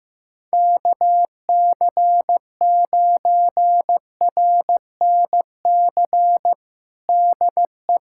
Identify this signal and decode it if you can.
Morse code, heard as a keyed tone, about 15 wpm, 700 Hz. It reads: KC9RNC DE
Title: USA Callsign DE Callsign 15wpm